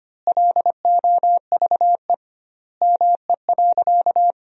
Morse code, heard as a keyed tone, about 25 wpm, 700 Hz. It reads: LOVE ME.
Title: 2x Word-Spacing - Sentences from Top 400 Words - 25wpm